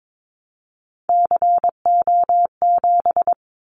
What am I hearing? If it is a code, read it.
Morse code, heard as a keyed tone, about 22 wpm, 700 Hz. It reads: CO7
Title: Callsign Prefixes for Top 100 DXCC - 22wpm